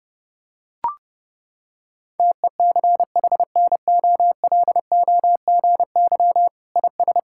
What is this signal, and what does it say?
Morse code, heard as a keyed tone, about 30 wpm, 700 Hz. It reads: TECHNOLOGY IS
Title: Sentences from Top 1000 Words - Review Part 5 - 30wpm